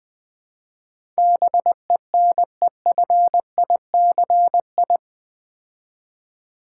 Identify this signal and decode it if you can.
Morse code, heard as a keyed tone, about 20 wpm, 700 Hz. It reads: BENEFICI
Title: Build A Word - 4 Letters At A Time - 11-Letter Words - 20wpm